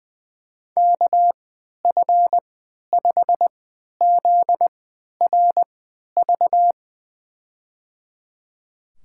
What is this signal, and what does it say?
Morse code, heard as a keyed tone, about 21 wpm, spaced as for 11 wpm, 700 Hz. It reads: KF5ZRV
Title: USA Call Signs - 6 Characters - 3x Character Spacing - 20wpm